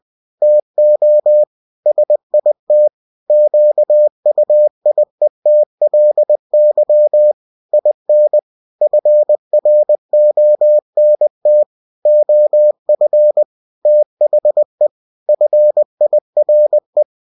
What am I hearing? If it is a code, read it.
Morse code, heard as a keyed tone, about 20 wpm, 600 Hz. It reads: TO SIT QUIETLY IN FRONT OF THE FIRE